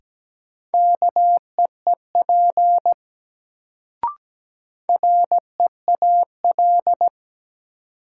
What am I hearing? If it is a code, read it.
Morse code, heard as a keyed tone, about 17 wpm, 700 Hz. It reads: KEEP REAL
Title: Sentences from Top 300 Words - New Word Drill - 17wpm